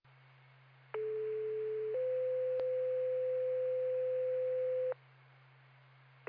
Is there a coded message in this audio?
Yes: two-tone page from 0.9 to 4.9 s, heard as a one-second tone, then a three-second tone, about 430 then 510 Hz.